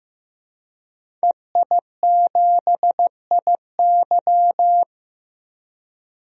Morse code, heard as a keyed tone, about 15 wpm, 700 Hz. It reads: EI7IY
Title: DXCC-100 Call Signs - 5 Characters - Repeated 3x - 15wpm